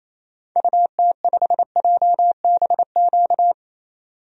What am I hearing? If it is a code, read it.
Morse code, heard as a keyed tone, about 28 wpm, 700 Hz. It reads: UT5JBQ
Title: DXCC-100 Call Signs - 6 Characters - 28wpm